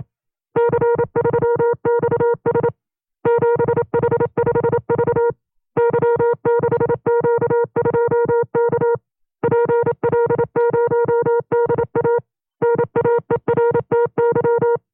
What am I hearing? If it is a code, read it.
Morse code, heard as a keyed tone, about 27 wpm, 470 Hz. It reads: C3XS 7H5V Y6Q2K PL0DA NAERTY
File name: 25WPM